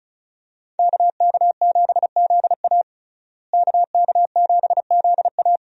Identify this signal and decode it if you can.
Morse code, heard as a keyed tone, about 36 wpm, 700 Hz. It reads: KK7ZA KK7ZA
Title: USA Call Signs - 5 Characters - Repeated 3x - 35wpm